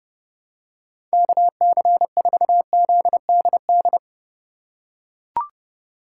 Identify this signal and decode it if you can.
Morse code, heard as a keyed tone, about 31 wpm, 700 Hz. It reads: KC4ZDD
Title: USA Call Signs - 6 Characters - 4x Character Spacing - 30wpm